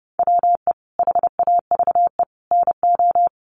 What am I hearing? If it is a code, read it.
Morse code, heard as a keyed tone, about 30 wpm, 700 Hz. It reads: WE HAVE NO
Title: Sentences from Top 100 Words 30wpm